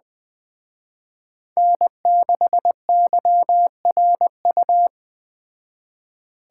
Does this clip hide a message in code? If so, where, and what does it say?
Yes: Morse code from 1.6 to 4.9 s, heard as a keyed tone, about 21 wpm, 700 Hz. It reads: N6YRU